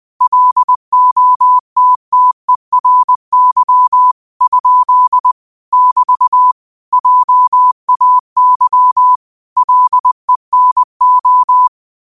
Morse code, heard as a keyed tone, about 20 wpm, 1000 Hz. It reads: LOTTERY ? <BT> JAY LENO
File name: qotdfiles / 20wpm